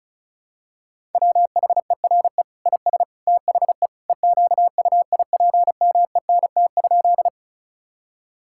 Morse code, heard as a keyed tone, about 34 wpm, 700 Hz. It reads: WHERE IS THE EQUIPMENT?
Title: Sentences from Top 1000 Words - Review Part 5 - 35wpm